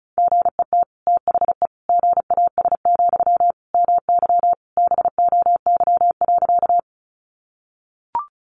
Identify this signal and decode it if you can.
Morse code, heard as a keyed tone, about 34 wpm, 700 Hz. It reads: GET THE GAS, MY BOY.
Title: Sentences from Top 1100 Words 35wpm